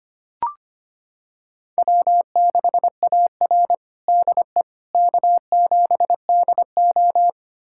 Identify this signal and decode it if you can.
Morse code, heard as a keyed tone, about 25 wpm, 700 Hz. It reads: W6AR DE K7DO
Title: USA Callsign DE Callsign - 4 Characters - 25wpm